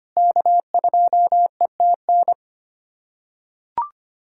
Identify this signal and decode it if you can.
Morse code, heard as a keyed tone, about 25 wpm, 700 Hz. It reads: K2ETN
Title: USA Callsign DE Callsign - 5 Characters - 25wpm